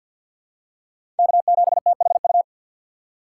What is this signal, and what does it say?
Morse code, heard as a keyed tone, about 51 wpm, 700 Hz. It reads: K7THU
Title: USA Call Signs - 5 Characters - 4x Character Spacing - 50wpm